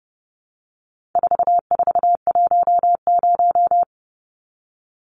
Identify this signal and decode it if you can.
Morse code, heard as a keyed tone, about 30 wpm, 700 Hz. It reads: 4410
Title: Numbers - 4-Digit 30wpm